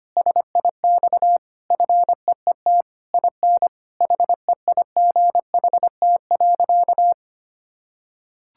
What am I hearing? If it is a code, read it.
Morse code, heard as a keyed tone, about 25 wpm, 700 Hz. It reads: SIX FEET IN HEIGHT.
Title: Sentences from Top 1600 Words 25wpm